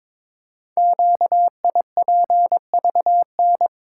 Morse code, heard as a keyed tone, about 22 wpm, 700 Hz. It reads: QIPVN
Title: Code Groups - 5 Letter:Numbers - 22wpm